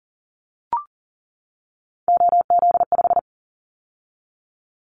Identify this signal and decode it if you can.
Morse code, heard as a keyed tone, about 42 wpm, 700 Hz. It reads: OZ5
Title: Callsign Prefixes for Top 50 DXCC - 40wpm